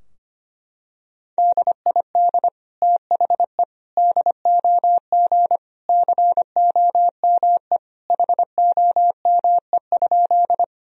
Morse code, heard as a keyed tone, about 25 wpm, 700 Hz. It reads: DID THE DOG COME HOME?
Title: Sentences from Top 1000 Words - Review Part 2 - 25wpm